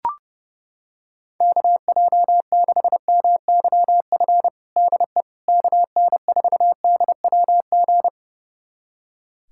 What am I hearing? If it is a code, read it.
Morse code, heard as a keyed tone, about 32 wpm, 700 Hz. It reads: KJ6MYF DE KN4DWG
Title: USA Callsign DE Callsign 30wpm